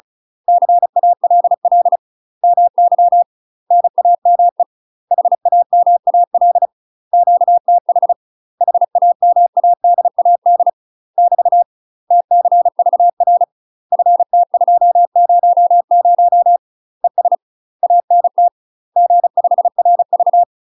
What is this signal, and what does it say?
Morse code, heard as a keyed tone, about 36 wpm, 700 Hz. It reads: CALL MY NAME HAMAL QTH HAMADAD <BT> TCVR FT200 ES ANT G5RV
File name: UK MASTER QSOs TRAINING / 35WPM / UK MASTER QSOs 0017